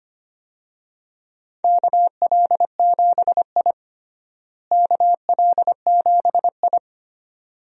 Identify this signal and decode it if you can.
Morse code, heard as a keyed tone, about 25 wpm, 700 Hz. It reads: KL7I KL7I
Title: DXCC-100 Call Signs - 4 Characters - Repeated 2x - 25wpm